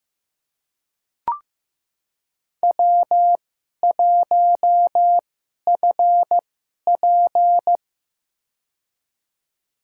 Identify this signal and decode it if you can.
Morse code, heard as a keyed tone, about 15 wpm, spaced as for 11 wpm, 700 Hz. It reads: W1FP
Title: USA Call Signs - 4 Characters - 2x Character Spacing - 15wpm